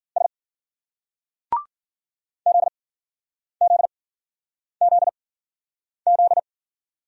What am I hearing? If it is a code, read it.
Morse code, heard as a keyed tone, about 50 wpm, spaced as for 9 wpm, 700 Hz. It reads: S ZZZZ